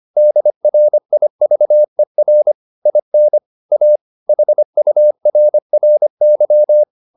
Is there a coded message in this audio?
Yes: Morse code from 0.2 to 6.8 s, heard as a keyed tone, about 25 wpm, 600 Hz. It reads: DRIVER IN A HURRY